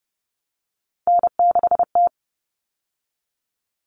Morse code, heard as a keyed tone, about 30 wpm, 700 Hz. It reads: N6T
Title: All Cut Numbers - 3-Digit 30wpm